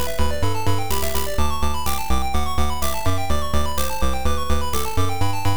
There is no code.